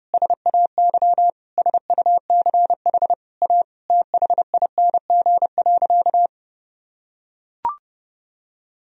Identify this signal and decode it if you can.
Morse code, heard as a keyed tone, about 32 wpm, 700 Hz. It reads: SAY SUCH A THING.
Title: Sentences from Top 200 Words and Abbreviations 30wpm